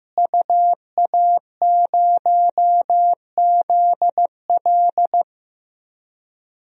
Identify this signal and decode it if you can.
Morse code, heard as a keyed tone, about 15 wpm, 700 Hz. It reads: UA0ZL
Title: DXCC-100 Call Signs - 5 Characters - Repeated 3x - 15wpm